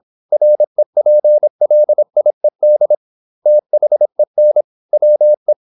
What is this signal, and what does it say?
Morse code, heard as a keyed tone, about 26 wpm, 600 Hz. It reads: REPLIED THEN WE
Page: Never 26wpm | CW med Gnister